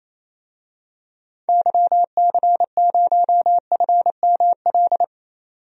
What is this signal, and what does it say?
Morse code, heard as a keyed tone, about 28 wpm, 700 Hz. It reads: YC0FML
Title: DXCC-100 Call Signs - 6 Characters - 28wpm